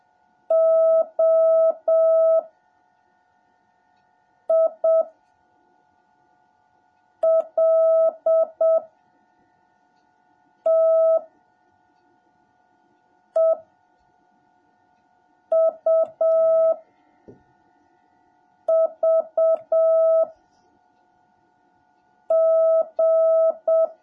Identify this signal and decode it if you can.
Morse code, heard as a keyed tone, about 7 wpm, 650 Hz. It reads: O I L T E U V G
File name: jh8cbh / 2021cw / cw17a